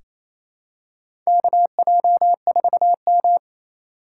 Morse code, heard as a keyed tone, about 28 wpm, 700 Hz. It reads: KJ4M
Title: USA Call Signs - 4 Characters - 3x Character Spacing - 28wpm